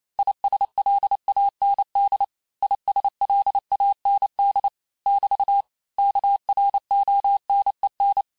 Morse code, heard as a keyed tone, about 28 wpm, 800 Hz. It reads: ISLAND ISLAND <BT> KRONEN
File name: cours-du-mercredi / audios / cours-865-teste-les-mots-vitesse-27mpm